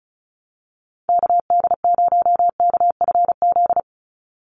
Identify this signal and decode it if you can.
Morse code, heard as a keyed tone, about 34 wpm, 700 Hz. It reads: KD0KFZ
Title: USA Call Signs 35wpm